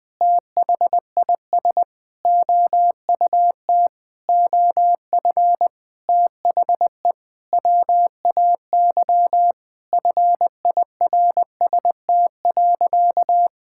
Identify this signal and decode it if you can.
Morse code, heard as a keyed tone, about 21 wpm, 700 Hz. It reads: THIS OUT OF THE WAY FIRST.
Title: Sentences with 7-Word Phrases - Spoken Beginning - 20wpm